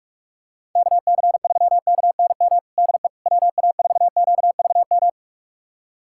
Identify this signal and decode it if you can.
Morse code, heard as a keyed tone, about 46 wpm, 700 Hz. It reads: KC3KNM DE WA4QVM
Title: USA Callsign DE Callsign - 6 Characters - 45wpm